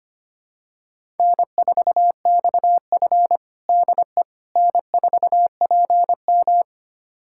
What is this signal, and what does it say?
Morse code, heard as a keyed tone, about 25 wpm, 700 Hz. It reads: N4XF DE N4PM
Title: USA Callsign DE Callsign - 4 Characters - 25wpm